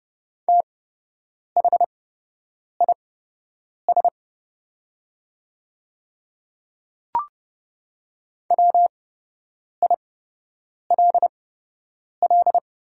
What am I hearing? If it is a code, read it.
Morse code, heard as a keyed tone, about 29 wpm, spaced as for 8 wpm, 700 Hz. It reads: THIS WILL
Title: 8x Character Spacing - Top 200 Words - 30wpm